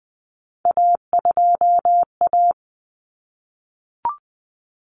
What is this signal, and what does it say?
Morse code, heard as a keyed tone, about 20 wpm, 700 Hz. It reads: A2A
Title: Common Cut Numbers - 3-Digit 20wpm